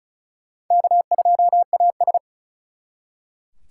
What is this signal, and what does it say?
Morse code, heard as a keyed tone, about 43 wpm, 700 Hz. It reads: K2AS
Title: Speed Racing - USA 1x2 Call Signs 35wpm